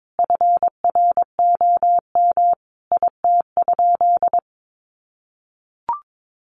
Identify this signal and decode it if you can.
Morse code, heard as a keyed tone, about 22 wpm, 700 Hz. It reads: FROM IT?
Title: Sentences from Top 800 Words 22wpm